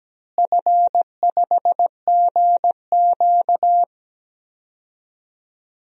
Morse code, heard as a keyed tone, about 17 wpm, 700 Hz. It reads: F5GQ
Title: DXCC-100 Call Signs - 4 Characters - Repeated 3x - 17wpm